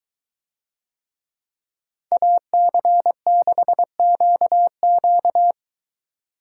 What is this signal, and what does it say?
Morse code, heard as a keyed tone, about 21 wpm, 700 Hz. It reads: AC6QQ